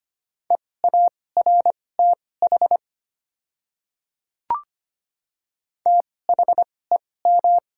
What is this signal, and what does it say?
Morse code, heard as a keyed tone, about 25 wpm, spaced as for 18 wpm, 700 Hz. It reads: EARTH THEM
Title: 2x Character Spacing - Top 200 Words - 25wpm